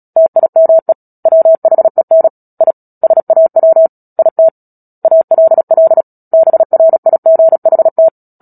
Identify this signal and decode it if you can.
Morse code, heard as a keyed tone, about 37 wpm, 650 Hz. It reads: TIME WHEN I SAW IT ALL BRIGHT
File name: Book Two / 10-Chapter 10 - Epilogue - 36 WPM